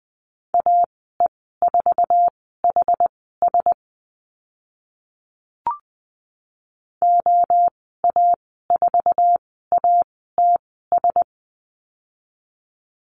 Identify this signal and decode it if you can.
Morse code, heard as a keyed tone, about 20 wpm, spaced as for 14 wpm, 700 Hz. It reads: AE4HS OA4ATS
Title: 069 Lesson 12g - Letter d - Call Signs 20wpm